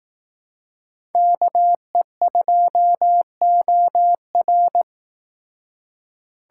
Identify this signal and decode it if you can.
Morse code, heard as a keyed tone, about 18 wpm, 700 Hz. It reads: KE2OR
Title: Speed Racing - USA 2x2 Call Signs 15wpm